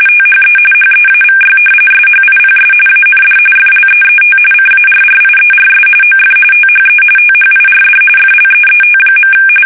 RTTY, about 45.45 baud. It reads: THE HEAVY SHIPS AND FLEET CARRIERS OF THE THIRD FLEET.
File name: fsk850 / battleshipnj